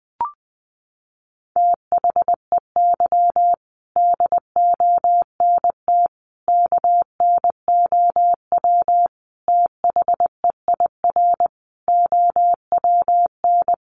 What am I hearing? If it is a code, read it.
Morse code, heard as a keyed tone, about 20 wpm, 700 Hz. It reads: THEY DONT KNOW THEIR OWN
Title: Sentences from Top 300 Words 20wpm